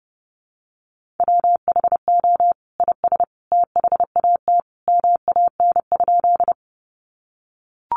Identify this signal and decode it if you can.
Morse code, heard as a keyed tone, about 30 wpm, 700 Hz. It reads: WHO IS THAT MAN?
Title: Sentences from Top 200 Words 30wpm